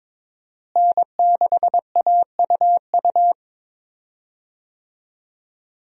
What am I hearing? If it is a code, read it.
Morse code, heard as a keyed tone, about 22 wpm, 700 Hz. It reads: N6AUU
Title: USA Call Signs - 5 Characters - Repeated 3x - 22wpm